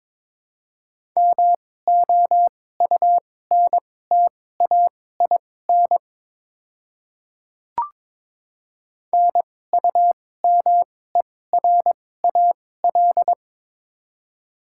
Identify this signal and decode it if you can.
Morse code, heard as a keyed tone, about 23 wpm, spaced as for 16 wpm, 700 Hz. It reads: MOUNTAIN NUMERAL
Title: 2x Character Spacing - Top 500 Words - 22wpm